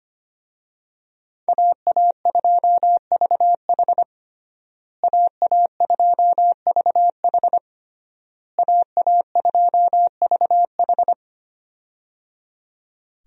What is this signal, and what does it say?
Morse code, heard as a keyed tone, about 25 wpm, 700 Hz. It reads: AA2VH AA2VH AA2VH
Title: USA Call Signs - 5 Characters - Repeated 3x - 25wpm